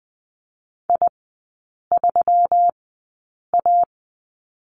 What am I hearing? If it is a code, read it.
Morse code, heard as a keyed tone, about 20 wpm, spaced as for 8 wpm, 700 Hz. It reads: I3A
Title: 184 Lesson 30e - Letter b - Three Characters 20wpm